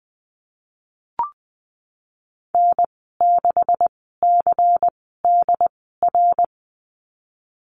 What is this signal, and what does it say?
Morse code, heard as a keyed tone, about 20 wpm, spaced as for 14 wpm, 700 Hz. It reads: N6CDR
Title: 147 Lesson 24g - Letter y - Call Signs 20wpm